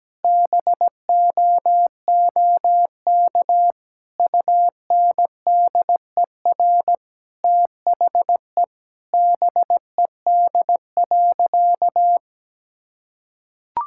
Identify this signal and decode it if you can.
Morse code, heard as a keyed tone, about 17 wpm, 700 Hz. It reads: BOOK UNDER THE BED.
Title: Sentences from Top 1000 Words - Review Part 6 - 17wpm